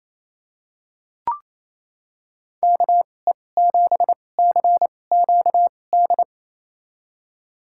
Morse code, heard as a keyed tone, about 28 wpm, spaced as for 20 wpm, 700 Hz. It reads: KE7CQD
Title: USA Call Signs - 6 Characters - 2x Character Spacing - 28wpm